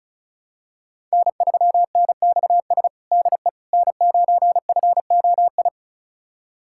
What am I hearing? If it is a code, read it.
Morse code, heard as a keyed tone, about 34 wpm, 700 Hz. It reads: N3NXS DE N9FOI
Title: USA Callsign DE Callsign - 5 Characters - 35wpm